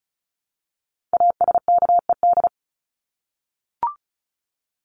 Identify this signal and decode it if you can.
Morse code, heard as a keyed tone, about 34 wpm, 700 Hz. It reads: ASKED
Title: ED Suffix - Top 100 Words with Base Words 35wpm